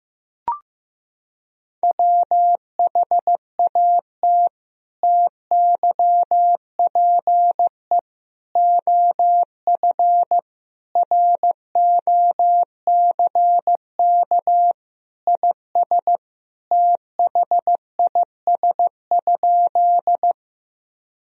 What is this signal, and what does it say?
Morse code, heard as a keyed tone, about 15 wpm, 700 Hz. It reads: WHAT TYPE OF ROCK IS THIS?
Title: Sentences from Top 1000 Words - Review Part 3 - 15wpm